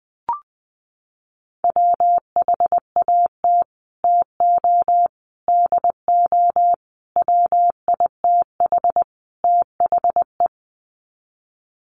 Morse code, heard as a keyed tone, about 20 wpm, 700 Hz. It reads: WHAT TO DO WITH THE
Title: Sets of 5 Words Encore2 20wpm